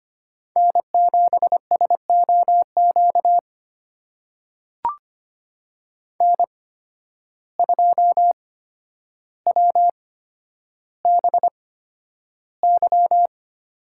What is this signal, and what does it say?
Morse code, heard as a keyed tone, about 25 wpm, spaced as for 11 wpm, 700 Hz. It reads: N7SOQ N2WBY
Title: USA Call Signs - 5 Characters - 8x Character Spacing - 25wpm